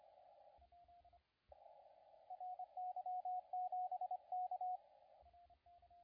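Morse code, heard as a keyed tone, about 25 wpm, 720 Hz, with radio noise. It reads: RY7K
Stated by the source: CW contest audio recorded at LA4C during the SAC CW contest.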